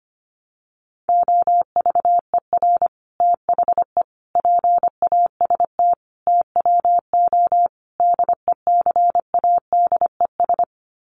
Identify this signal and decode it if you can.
Morse code, heard as a keyed tone, about 25 wpm, 700 Hz. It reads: OVER THE PAST TWO DECADES
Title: Sets of 5 Words Encore2 25wpm